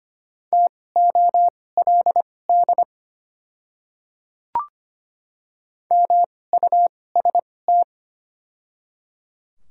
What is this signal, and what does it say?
Morse code, heard as a keyed tone, about 25 wpm, spaced as for 18 wpm, 700 Hz. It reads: TOLD MUST
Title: 2x Character Spacing - Top 500 Words - 25wpm